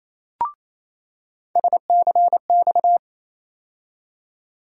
Morse code, heard as a keyed tone, about 28 wpm, 700 Hz. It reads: SCX
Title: USA Call Sign Postfixes - 3 Letters 28wpm